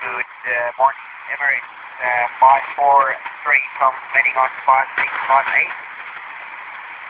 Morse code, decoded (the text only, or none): none